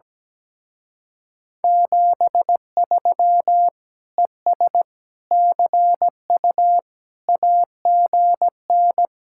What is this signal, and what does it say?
Morse code, heard as a keyed tone, about 17 wpm, 700 Hz. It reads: 73 ES CU AGN
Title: Speed-Racing Common QSO Phrases - 17wpm